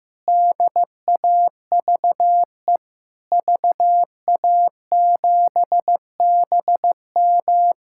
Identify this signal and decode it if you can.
Morse code, heard as a keyed tone, about 15 wpm, 700 Hz. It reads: DAVE VA7BM
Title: K1USN SST QSO Practice - 15wpm